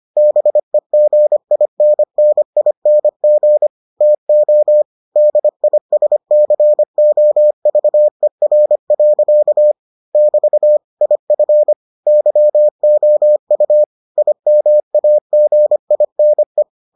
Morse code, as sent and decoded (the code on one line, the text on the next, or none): -... --. .. -. -. .. -. --. / - --- / -.. .. ... -.-. --- ...- . .-. .-.-.- / -...- / .. ..-. / -.-- --- ..- / .. -- .- --. .. -. .
BEGINNING TO DISCOVER. <BT> IF YOU IMAGINE